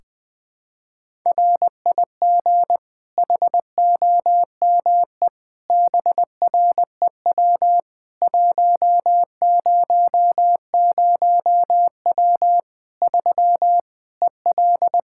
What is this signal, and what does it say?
Morse code, heard as a keyed tone, about 21 wpm, 700 Hz. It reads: RIG HOME BREW 100W 3 EL
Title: Practice QSO - Exchange 2 - Info - 20wpm